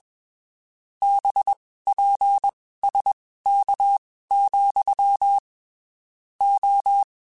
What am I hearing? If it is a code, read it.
Morse code, heard as a keyed tone, about 21 wpm, spaced as for 15 wpm, 800 Hz. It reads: BPSK, O
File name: (spaced) / QSOs Spaced-mp3s / WA7QOX 2 @20 sp